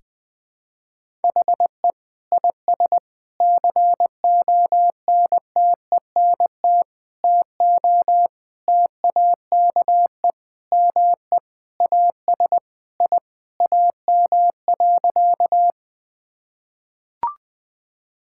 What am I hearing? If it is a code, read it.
Morse code, heard as a keyed tone, about 20 wpm, 700 Hz. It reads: HE IS CONTENT TO TAKE ME AS I AM.
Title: Sentences from Top 1000 Words - Review Part 6 - 20wpm